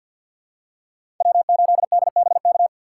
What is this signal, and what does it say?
Morse code, heard as a keyed tone, about 53 wpm, 700 Hz. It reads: W8DBK
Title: USA Call Signs - 5 Characters - 6x Character Spacing - 50wpm